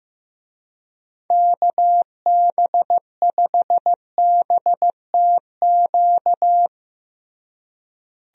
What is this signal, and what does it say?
Morse code, heard as a keyed tone, about 15 wpm, 700 Hz. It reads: KB5BTQ
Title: USA Call Signs - 6 Characters - Repeated 3x - 15wpm